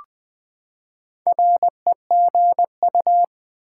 Morse code, heard as a keyed tone, about 20 wpm, 700 Hz. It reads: REGU
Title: Build A Word - 4 Letters At A Time - 8-Letter Words - 20wpm